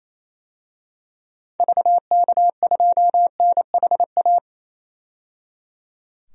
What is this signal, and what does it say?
Morse code, heard as a keyed tone, about 28 wpm, 700 Hz. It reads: VK2NHA
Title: DXCC-100 Call Signs - 6 Characters - 28wpm